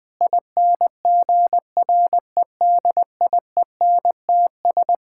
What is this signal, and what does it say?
Morse code, heard as a keyed tone, about 21 wpm, 700 Hz. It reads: INGREDIENTS
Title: Sentences with 5-Word Phrases - 20wpm